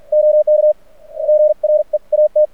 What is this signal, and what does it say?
Morse code, heard as a keyed tone, about 46 wpm, spaced as for 40 wpm, 610 Hz. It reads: OK GREAT
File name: archived / CRSnet